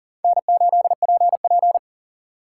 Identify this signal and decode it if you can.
Morse code, heard as a keyed tone, about 40 wpm, 700 Hz. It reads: N8PP
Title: USA Call Signs - 4 Characters - 4x Character Spacing - 40wpm